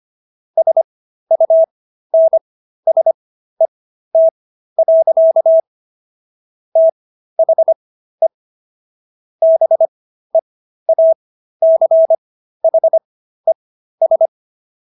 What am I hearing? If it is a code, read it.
Morse code, heard as a keyed tone, about 25 wpm, spaced as for 13 wpm, 650 Hz. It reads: SUNSET. THE BEACHES